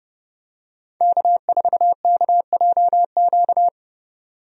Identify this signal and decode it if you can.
Morse code, heard as a keyed tone, about 29 wpm, 700 Hz. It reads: K4KJQ
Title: USA Call Signs - 5 Characters - 30wpm